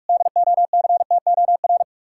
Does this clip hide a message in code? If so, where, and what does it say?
Yes: Morse code from 0.1 to 1.8 s, heard as a keyed tone, about 58 wpm, 700 Hz. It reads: DOCTOR